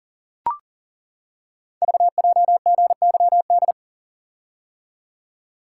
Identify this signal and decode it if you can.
Morse code, heard as a keyed tone, about 40 wpm, 700 Hz. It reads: VJGYD
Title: Code Groups - 5 Letters - 40wpm